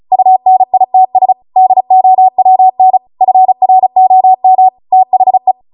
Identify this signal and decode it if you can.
Morse code, heard as a keyed tone, about 36 wpm, 750 Hz. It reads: UNITS DOWN FROM THE
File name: Morse / 35WPM